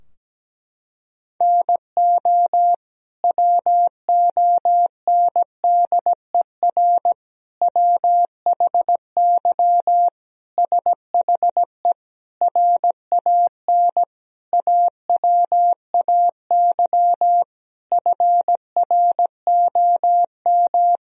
Morse code, heard as a keyed tone, about 17 wpm, 700 Hz. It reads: NO WONDER WHY SHE RAN AWAY FROM
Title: Sentences from Top 1000 Words - Review Part 3 - 17wpm